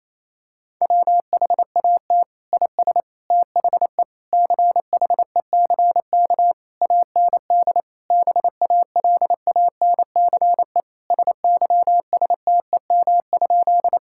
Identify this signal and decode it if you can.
Morse code, heard as a keyed tone, about 27 wpm, 700 Hz. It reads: WHAT IS THE CHECK AND BALANCE SYSTEM?
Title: Sentences from Top 1000 Words - Review Part 6 - 28wpm